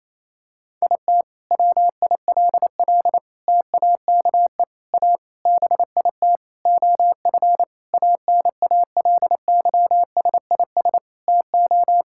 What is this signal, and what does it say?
Morse code, heard as a keyed tone, about 28 wpm, 700 Hz. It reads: IT WILL TAKE A BIT OF ANALYSIS TO
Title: Sentences from Top 1000 Words - Review Part 5 - 28wpm